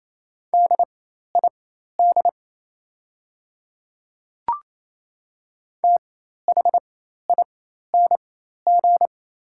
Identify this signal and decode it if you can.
Morse code, heard as a keyed tone, about 28 wpm, spaced as for 13 wpm, 690 Hz. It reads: DID THING
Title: 4x Character Spacing - Top 200 Words - 28wpm